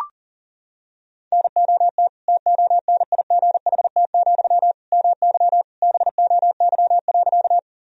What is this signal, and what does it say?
Morse code, heard as a keyed tone, about 37 wpm, 700 Hz. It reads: NOT TONIGHT, MY BOY.
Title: Sentences from Top 1000 Words - Review Part 3 - 40wpm